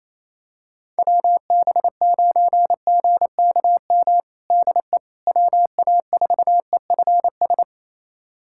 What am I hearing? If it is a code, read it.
Morse code, heard as a keyed tone, about 28 wpm, 700 Hz. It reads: WB9GKM DE WA4EFS
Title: USA Callsign DE Callsign - 6 Characters - 28wpm